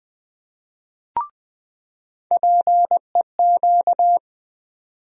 Morse code, heard as a keyed tone, about 20 wpm, 700 Hz. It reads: PEQ
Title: USA Call Sign Postfixes - 3 Letters 20wpm